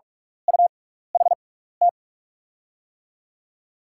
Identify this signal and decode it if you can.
Morse code, heard as a keyed tone, about 44 wpm, spaced as for 15 wpm, 700 Hz. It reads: UHT
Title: USA Call Signs - 6x Character Spacing - 45wpm